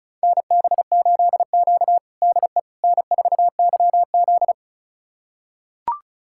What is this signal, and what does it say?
Morse code, heard as a keyed tone, about 36 wpm, 700 Hz. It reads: NB8Q DE N4YZ
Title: USA Callsign DE Callsign - 4 Characters - 35wpm